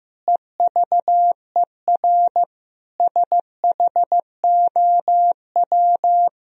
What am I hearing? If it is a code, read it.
Morse code, heard as a keyed tone, about 15 wpm, 700 Hz. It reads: EVER SHOW
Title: Sentences from Top 1000 Words - Review Part 2 - 15wpm